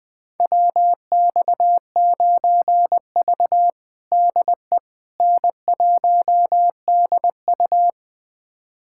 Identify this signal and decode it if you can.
Morse code, heard as a keyed tone, about 21 wpm, 700 Hz. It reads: WX9V DE N1DU
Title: USA Callsign DE Callsign - 4 Characters - 20wpm